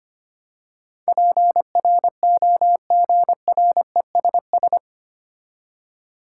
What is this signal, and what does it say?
Morse code, heard as a keyed tone, about 25 wpm, 700 Hz. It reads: PROGRESS
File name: Eight Letter Words - 25wpm